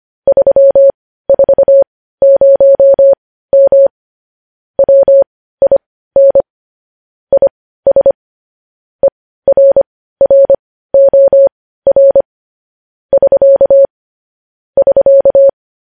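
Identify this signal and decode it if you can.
Morse code, heard as a keyed tone, about 25 wpm, spaced as for 15 wpm, 550 Hz. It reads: 340M WIN IS ERROR <SK> <SK>
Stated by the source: Morse code news headlines sent at 15 wpm.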